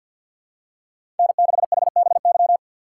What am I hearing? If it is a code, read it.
Morse code, heard as a keyed tone, about 53 wpm, 700 Hz. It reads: N6HBY
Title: USA Call Signs - 5 Characters - 6x Character Spacing - 50wpm